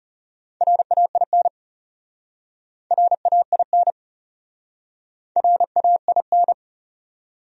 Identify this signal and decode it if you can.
Morse code, heard as a keyed tone, about 34 wpm, 700 Hz. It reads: RAIN RAIN RAIN
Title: Speed-Racing Top 500 Words 30wpm